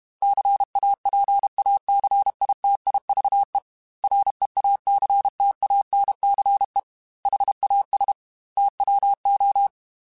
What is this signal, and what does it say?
Morse code, heard as a keyed tone, about 32 wpm, 800 Hz. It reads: CAPACITIVE REACTANCE HAS TWO
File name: Technical Subjects / Technical Subjects-mp3s / XC @30